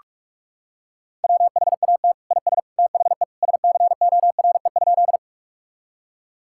Morse code, heard as a keyed tone, about 46 wpm, 700 Hz. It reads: WHAT IS THE SCORE?
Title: Sentences from Top 1600 Words 45wpm